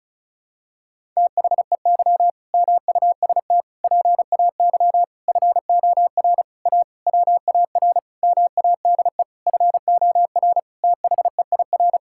Morse code, heard as a keyed tone, about 36 wpm, 700 Hz. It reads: THEY MUST PAY FOR A WAR MADE FOR THEIR
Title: Sentences from Top 1000 Words - Review Part 6 - 35wpm